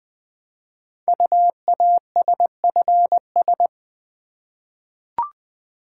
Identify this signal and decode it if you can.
Morse code, heard as a keyed tone, about 20 wpm, 700 Hz. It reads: UASFS
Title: Code Groups - 5 Letters - 20wpm